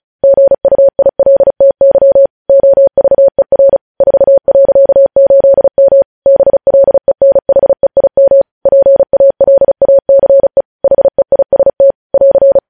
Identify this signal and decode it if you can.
Morse code, heard as a keyed tone, about 36 wpm, 550 Hz. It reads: GUILTY OVER 4.8M BLENHEIM PALACE HEIST <AR>
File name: podcast / News headlines-1x-35x35